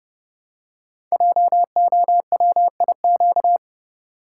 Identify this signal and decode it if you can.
Morse code, heard as a keyed tone, about 28 wpm, 700 Hz. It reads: JOWIQ
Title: Code Groups - 5 Letters - 30wpm